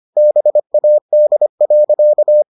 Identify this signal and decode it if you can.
Morse code, heard as a keyed tone, about 25 wpm, 600 Hz. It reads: BAD.